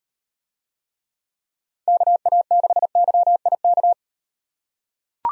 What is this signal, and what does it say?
Morse code, heard as a keyed tone, about 38 wpm, 700 Hz. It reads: KA6YIK